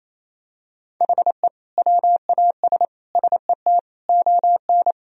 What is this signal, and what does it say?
Morse code, heard as a keyed tone, about 28 wpm, 700 Hz. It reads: HE WAS SET ON
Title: Sentences from Top 1000 Words - Review Part 1 - 28wpm